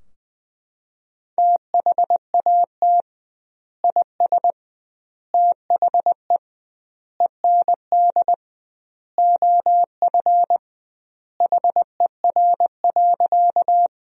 Morse code, heard as a keyed tone, about 20 wpm, 700 Hz. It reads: THAT IS THE END OF HER.
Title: 2x Word-Spacing - Sentences from Top 200 Words - 20wpm